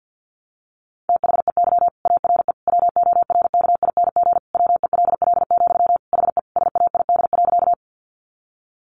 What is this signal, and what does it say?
Morse code, heard as a keyed tone, about 50 wpm, 700 Hz. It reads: THEY ARE WORKING WELL, HE SAID.
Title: Sentences from Top 700 Words 50wpm